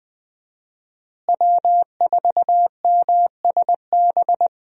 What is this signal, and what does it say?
Morse code, heard as a keyed tone, about 20 wpm, 700 Hz. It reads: W4MSB
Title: USA Call Signs - 5 Characters - Repeated 2x - 20wpm